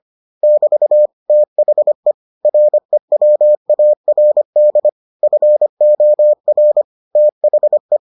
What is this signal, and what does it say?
Morse code, heard as a keyed tone, about 25 wpm, 600 Hz. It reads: <BT> THE REWARD FOR THE